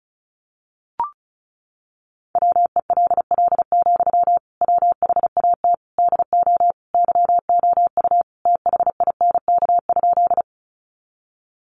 Morse code, heard as a keyed tone, about 34 wpm, 700 Hz. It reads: WELL, WHAT DO YOU THINK?
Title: Sentences from Top 200 Words 35wpm